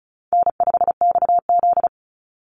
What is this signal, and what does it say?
Morse code, heard as a keyed tone, about 34 wpm, 700 Hz. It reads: N5XZ
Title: Easy Call Signs Repeated 2x - 35wpm